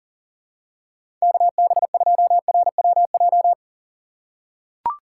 Morse code, heard as a keyed tone, about 40 wpm, 700 Hz. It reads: KB2RWJ
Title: USA Call Signs - 6x Character Spacing - 40wpm